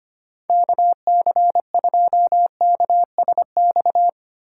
Morse code, heard as a keyed tone, about 25 wpm, 700 Hz. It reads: KC2KSX
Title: USA Call Signs - 6 Characters - 3x Character Spacing - 25wpm